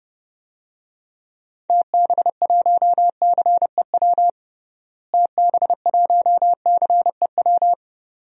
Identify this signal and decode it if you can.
Morse code, heard as a keyed tone, about 28 wpm, 700 Hz. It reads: TB1CEW TB1CEW
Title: DXCC-100 Call Signs - 6 Characters - Repeated 2x - 30wpm